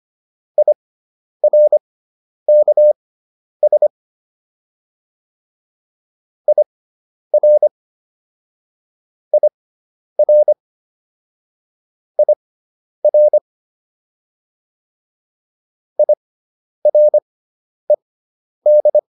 Morse code, heard as a keyed tone, about 25 wpm, spaced as for 10 wpm, 600 Hz. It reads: IRKS IR IR IR IRED